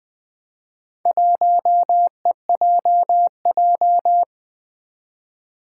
Morse code, heard as a keyed tone, about 20 wpm, 700 Hz. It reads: 1EJJ
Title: Four Challenging Letter-Number Combinations 20wpm